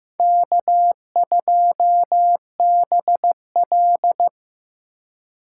Morse code, heard as a keyed tone, about 18 wpm, 700 Hz. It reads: K2BL
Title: Speed Racing - USA Call Signs 15wpm